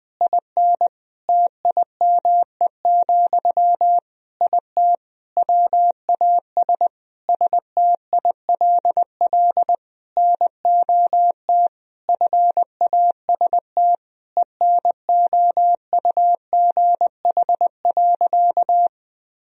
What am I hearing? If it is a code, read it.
Morse code, heard as a keyed tone, about 20 wpm, 700 Hz. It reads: IN TIME, IT WAS STILL NOT FAST ENOUGH.
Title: Sentences from Top 1000 Words - Review Part 3 - 20wpm